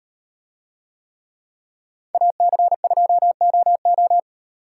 Morse code, heard as a keyed tone, about 36 wpm, 700 Hz. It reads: AC2OO